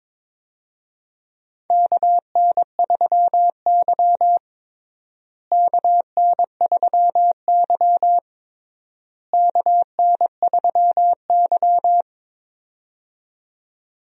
Morse code, heard as a keyed tone, about 22 wpm, 700 Hz. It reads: KN3Y KN3Y KN3Y
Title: USA Call Signs - 4 Characters - Repeated 3x - 22wpm